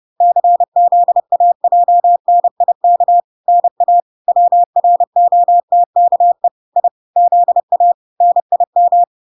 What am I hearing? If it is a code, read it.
Morse code, heard as a keyed tone, about 30 wpm, 700 Hz. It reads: CZAJNIK NA WROTKE I ZA NIM